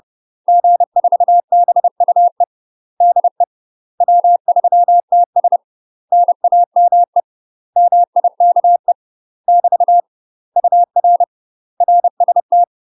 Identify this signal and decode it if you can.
Morse code, heard as a keyed tone, about 30 wpm, 700 Hz. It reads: G4BUE DE W3TS NAME MIKE <BT> UR RST
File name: UK MASTER QSOs TRAINING / 30WPM / UK MASTER QSOs 0021